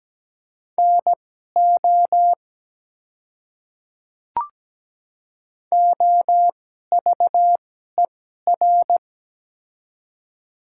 Morse code, heard as a keyed tone, about 17 wpm, spaced as for 12 wpm, 690 Hz. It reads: NO OVER
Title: 2x Character Spacing - Top 100 Words - 17wpm